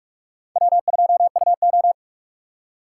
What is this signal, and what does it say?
Morse code, heard as a keyed tone, about 46 wpm, 700 Hz. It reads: W2UO
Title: USA Call Signs - 6x Character Spacing - 45wpm